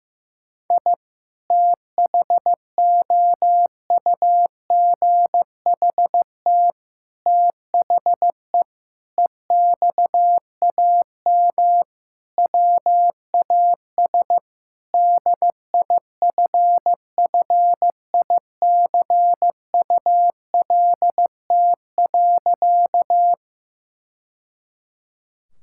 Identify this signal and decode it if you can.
Morse code, heard as a keyed tone, about 15 wpm, 700 Hz. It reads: I THOUGHT THE EXAM WAS DIFFICULT.
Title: Sentences from Top 1000 Words - Review Part 5 - 15wpm